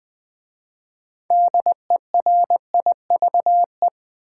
Morse code, heard as a keyed tone, about 20 wpm, 700 Hz. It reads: DERIVE
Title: 6 Letter Words - First 3 Letters Given - 20wpm